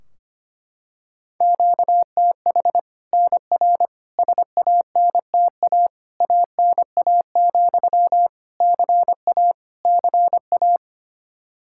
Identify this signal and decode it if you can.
Morse code, heard as a keyed tone, about 25 wpm, 700 Hz. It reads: QTH NR SANTA ANA, CA CA
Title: Practice QSO - Exchange 1 - Beginning - 25wpm